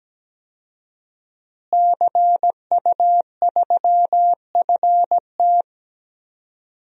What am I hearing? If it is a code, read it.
Morse code, heard as a keyed tone, about 17 wpm, 700 Hz. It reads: CU3FT
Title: DXCC-100 Call Signs - 5 Characters - Repeated 3x - 17wpm